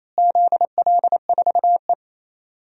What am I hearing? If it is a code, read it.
Morse code, heard as a keyed tone, about 28 wpm, 700 Hz. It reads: ZL4E
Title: DXCC-100 Call Signs - 4 Characters - Repeated 2x - 28wpm